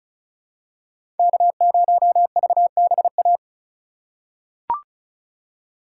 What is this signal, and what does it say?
Morse code, heard as a keyed tone, about 36 wpm, 700 Hz. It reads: K0VBA
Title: USA Call Signs - 4x Character Spacing - 35wpm